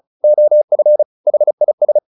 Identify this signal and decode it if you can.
Morse code, heard as a keyed tone, about 34 wpm, 600 Hz. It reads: OF HIS
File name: the old man and the sea 2 35wpm